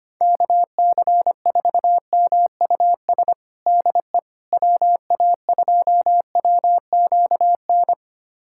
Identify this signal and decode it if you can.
Morse code, heard as a keyed tone, about 25 wpm, 700 Hz. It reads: KC4MUS DE WA2WQN
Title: USA Callsign DE Callsign 25wpm